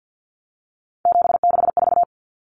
Morse code, heard as a keyed tone, about 50 wpm, 700 Hz. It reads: 764
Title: Numbers - 3-Digit 50wpm